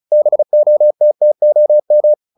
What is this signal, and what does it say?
Morse code, heard as a keyed tone, about 35 wpm, 600 Hz. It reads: BOTTOM
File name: the old man and the sea 2 35wpm